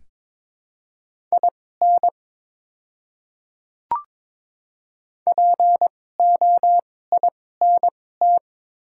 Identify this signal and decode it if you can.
Morse code, heard as a keyed tone, about 22 wpm, spaced as for 16 wpm, 700 Hz. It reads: IN POINT
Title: 2x Character Spacing - Top 200 Words - 22wpm